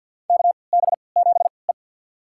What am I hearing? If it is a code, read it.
Morse code, heard as a keyed tone, about 48 wpm, spaced as for 28 wpm, 700 Hz. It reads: KB7E
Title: USA Call Signs - 4 Characters - 3x Character Spacing - 50wpm